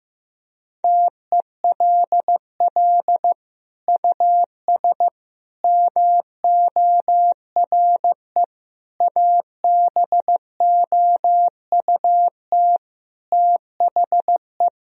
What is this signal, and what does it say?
Morse code, heard as a keyed tone, about 15 wpm, 700 Hz. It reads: TELL US MORE ABOUT THE
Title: Sentences from Top 1000 Words - Review Part 4 - 15wpm